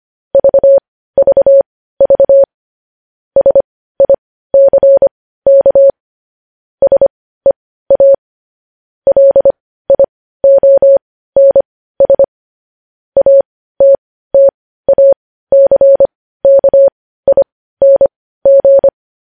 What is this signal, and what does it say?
Morse code, heard as a keyed tone, about 25 wpm, spaced as for 15 wpm, 550 Hz. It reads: VVV SICK SEA LIONS ATTACKING
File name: podcast / News headlines-1x-25x15